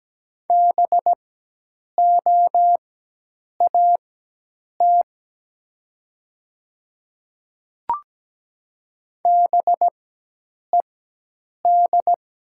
Morse code, heard as a keyed tone, about 17 wpm, spaced as for 8 wpm, 700 Hz. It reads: BOAT BED
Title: 4x Character Spacing - Top 500 Words - 17wpm